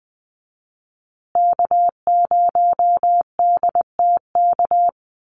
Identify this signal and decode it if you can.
Morse code, heard as a keyed tone, about 20 wpm, 700 Hz. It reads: K0DTK
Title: USA Call Signs Repeated 3x 20wpm